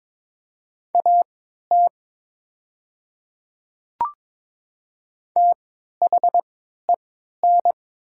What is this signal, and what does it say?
Morse code, heard as a keyed tone, about 22 wpm, spaced as for 12 wpm, 700 Hz. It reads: AT THEN
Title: 3x Character Spacing - Top 100 Words - 22wpm